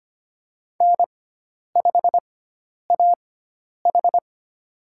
Morse code, heard as a keyed tone, about 25 wpm, spaced as for 10 wpm, 700 Hz. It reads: N5AH
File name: Lesson 16g - Letter c - Call Signs 25wpm